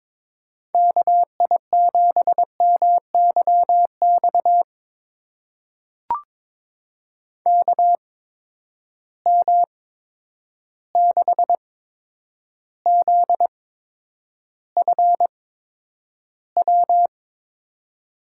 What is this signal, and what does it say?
Morse code, heard as a keyed tone, about 22 wpm, spaced as for 9 wpm, 700 Hz. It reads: KI7MYX KM6ZFW
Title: USA Call Signs - 6 Characters - 8x Character Spacing - 22wpm